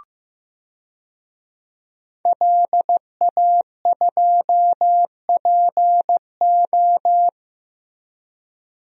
Morse code, heard as a keyed tone, about 15 wpm, 700 Hz. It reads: LA2PO
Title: DXCC-100 Call Signs - 5 Characters - Repeated 3x - 15wpm